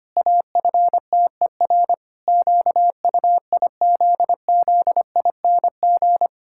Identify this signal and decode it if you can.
Morse code, heard as a keyed tone, about 25 wpm, 700 Hz. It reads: AFTER QUIZZING
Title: Pangram Challenge v2 - 25wpm